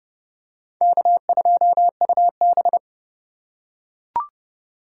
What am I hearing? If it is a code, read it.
Morse code, heard as a keyed tone, about 29 wpm, 700 Hz. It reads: K2UB